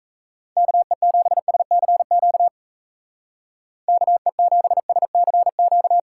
Morse code, heard as a keyed tone, about 33 wpm, 700 Hz. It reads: KE7SCQ KE7SCQ
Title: Speed Racing - USA 2x3 Call Signs 28wpm